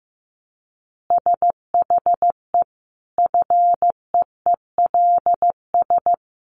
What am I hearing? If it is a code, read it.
Morse code, heard as a keyed tone, about 15 wpm, 700 Hz. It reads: SHE FEELS
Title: Additional Easy Phrases 15wpm